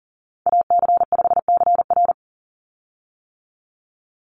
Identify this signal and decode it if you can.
Morse code, heard as a keyed tone, about 40 wpm, 700 Hz. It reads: AC5CR
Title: USA Call Signs 40wpm